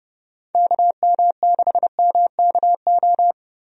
Morse code, heard as a keyed tone, about 32 wpm, 700 Hz. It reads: KM6MKO
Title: USA Call Signs - 6 Characters - Repeated 3x - 30wpm